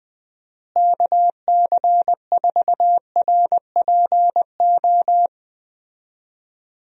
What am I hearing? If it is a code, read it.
Morse code, heard as a keyed tone, about 20 wpm, 700 Hz. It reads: KC4RPO
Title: USA Call Signs - 6 Characters - Repeated 3x - 20wpm